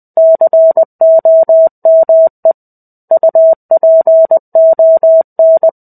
Morse code, heard as a keyed tone, about 20 wpm, 650 Hz. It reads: COME UPON
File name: 05-Chapter-05 20 WPM